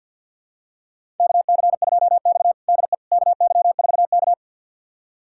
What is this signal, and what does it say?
Morse code, heard as a keyed tone, about 48 wpm, 700 Hz. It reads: KC2X DE KY4K
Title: USA Callsign DE Callsign - 4 Characters - 50wpm